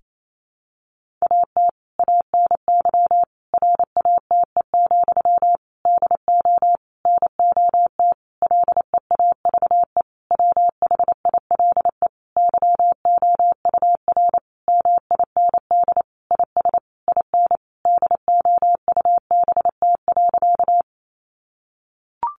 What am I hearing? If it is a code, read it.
Morse code, heard as a keyed tone, about 28 wpm, 700 Hz. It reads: AT ANY RATE, DO NOT LEAVE WHILE YOUR MIND IS IN DOUBT.
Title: Sentences from Top 1500 Words 28wpm